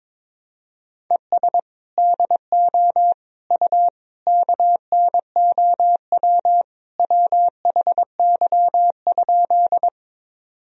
Morse code, heard as a keyed tone, about 22 wpm, 700 Hz. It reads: ES DO U KNOW WHY?
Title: Sentences from Top 200 Words and Abbreviations 22wpm